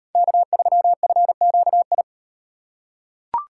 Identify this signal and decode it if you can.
Morse code, heard as a keyed tone, about 37 wpm, 700 Hz. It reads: K3FQI